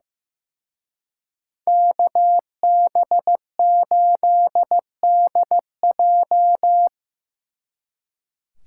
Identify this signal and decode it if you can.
Morse code, heard as a keyed tone, about 18 wpm, 700 Hz. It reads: KB8DJ
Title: Speed Racing - USA 2x2 Call Signs 15wpm